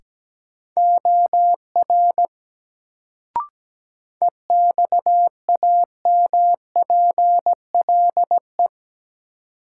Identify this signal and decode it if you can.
Morse code, heard as a keyed tone, about 17 wpm, 700 Hz. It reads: OR EXAMPLE
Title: Sets of 2 Words - New Word Drill - 17wpm